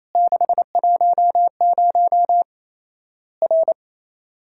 Morse code, heard as a keyed tone, about 28 wpm, 700 Hz. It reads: 610 R
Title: Worked All Europe DX Contest - QTC Exchange - 28wpm